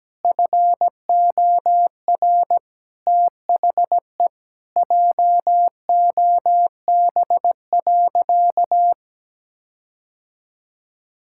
Morse code, heard as a keyed tone, about 17 wpm, 700 Hz. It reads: FOR THE JOB.
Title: Sentences from Top 1600 Words 17wpm